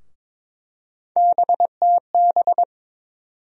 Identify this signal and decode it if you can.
Morse code, heard as a keyed tone, about 21 wpm, 700 Hz. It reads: BTB
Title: USA Call Sign Postfixes - 3 Letters 22wpm